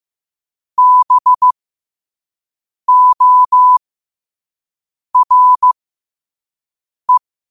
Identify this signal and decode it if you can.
Morse code, heard as a keyed tone, about 15 wpm, spaced as for 5 wpm, 1000 Hz. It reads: BORE
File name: qotdfiles / 5wpm